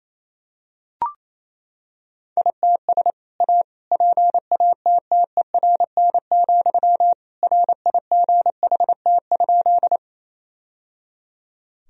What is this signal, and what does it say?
Morse code, heard as a keyed tone, about 27 wpm, 700 Hz. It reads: ITS A PATTERN, RIGHT?
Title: Sentences from Top 1000 Words - Review Part 2 - 28wpm